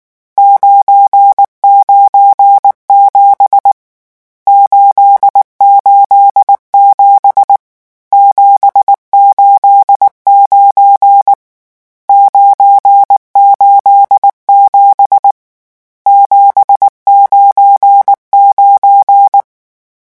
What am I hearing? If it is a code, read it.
Morse code, heard as a keyed tone, about 19 wpm, 800 Hz. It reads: 997 887 789 987 799
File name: morse mp3 / lecon12-vitesse 18